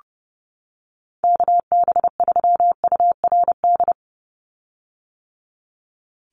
Morse code, heard as a keyed tone, about 30 wpm, 700 Hz. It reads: KB3URD
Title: USA Call Signs 30wpm